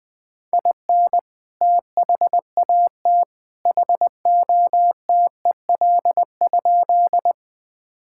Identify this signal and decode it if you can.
Morse code, heard as a keyed tone, about 20 wpm, 700 Hz. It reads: IN THAT HOTEL?
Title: Sentences from Top 1600 Words 20wpm